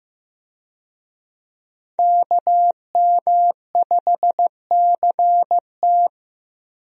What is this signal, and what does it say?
Morse code, heard as a keyed tone, about 15 wpm, 700 Hz. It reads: KM5CT
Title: USA Call Signs - 5 Characters - Repeated 2x - 15wpm